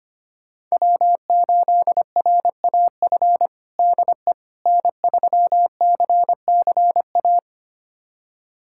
Morse code, heard as a keyed tone, about 25 wpm, 700 Hz. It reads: W8RAF DE N3CCA
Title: USA Callsign DE Callsign - 5 Characters - 25wpm